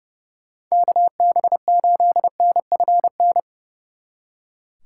Morse code, heard as a keyed tone, about 40 wpm, 700 Hz. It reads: KB8NFN